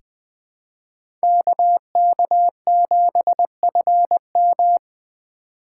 Morse code, heard as a keyed tone, about 20 wpm, 700 Hz. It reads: KK7FM
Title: USA Call Signs - 5 Characters - 2x Character Spacing - 20wpm